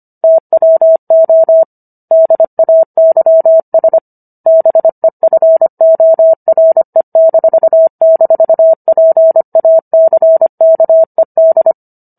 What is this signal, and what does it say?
Morse code, heard as a keyed tone, about 25 wpm, 650 Hz. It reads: TWO DAYS BEFORE--PACKED